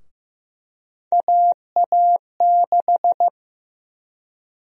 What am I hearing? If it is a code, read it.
Morse code, heard as a keyed tone, about 15 wpm, 700 Hz. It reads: AA6
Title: USA Call Sign Prefixes - 15wpm